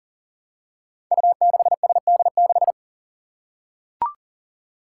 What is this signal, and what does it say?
Morse code, heard as a keyed tone, about 39 wpm, 700 Hz. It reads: U6SD6
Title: Five Challenging Letter-Number Combinations 40wpm